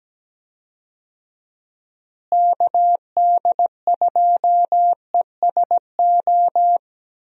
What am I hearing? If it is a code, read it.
Morse code, heard as a keyed tone, about 17 wpm, 700 Hz. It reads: KD2ESO
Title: USA Call Signs - 6 Characters - Repeated 3x - 17wpm